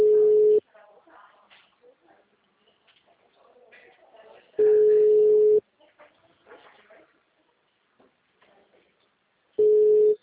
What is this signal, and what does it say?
Ringback tone (Europe, 425 Hz), with speech.
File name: out- -2005- - - .82835